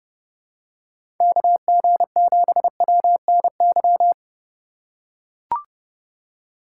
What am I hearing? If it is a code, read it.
Morse code, heard as a keyed tone, about 32 wpm, 700 Hz. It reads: KG7WNY
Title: USA Call Signs - 6 Characters - 4x Character Spacing - 30wpm